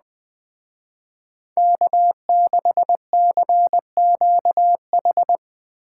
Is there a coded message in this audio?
Yes: Morse code from 1.6 to 5.4 s, heard as a keyed tone, about 20 wpm, 700 Hz. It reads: K6CQH